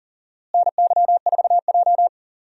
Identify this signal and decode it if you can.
Morse code, heard as a keyed tone, about 40 wpm, 700 Hz. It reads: NY4J
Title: USA Call Signs - 4 Characters - Repeated 3x - 40wpm